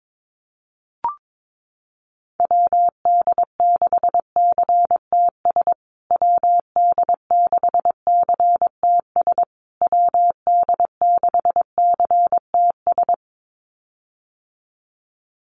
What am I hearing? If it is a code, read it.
Morse code, heard as a keyed tone, about 22 wpm, 700 Hz. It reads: WD6CTS WD6CTS WD6CTS
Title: USA Call Signs Repeated 3x 22wpm